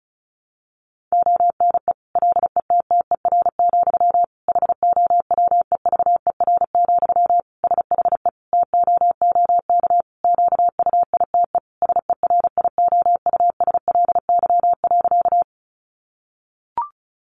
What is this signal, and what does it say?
Morse code, heard as a keyed tone, about 34 wpm, 700 Hz. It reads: ONE LETTER, HOWEVER, SHE TOOK QUITE SERIOUSLY.
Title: Sentences from Top 1200 Words 35wpm